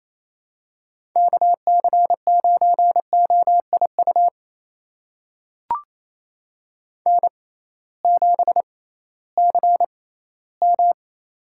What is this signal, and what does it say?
Morse code, heard as a keyed tone, about 28 wpm, spaced as for 16 wpm, 700 Hz. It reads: KC9OIU N7CM
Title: USA Call Signs - 6x Character Spacing - 28wpm